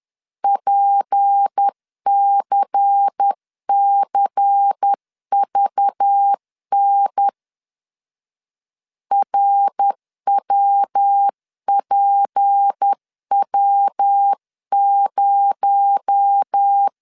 Morse code, heard as a keyed tone, about 10 wpm, 800 Hz. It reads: PCCVN RWPW0
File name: CW / DET / 1 6 1500C S C 10wpm 4252